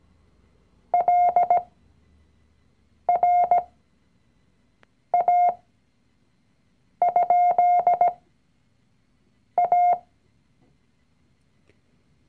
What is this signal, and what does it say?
Morse code, heard as a keyed tone, about 17 wpm, spaced as for 5 wpm, 710 Hz. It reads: LRA?A
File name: CW-Online-Kurs / Anfänger / Lektion 019